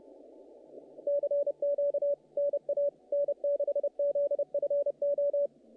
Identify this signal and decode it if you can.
Morse code, heard as a keyed tone, about 30 wpm, 570 Hz. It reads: CQ NA N6ZFO
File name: contests / naqp-cw-k5tr-op-k5pi- -2100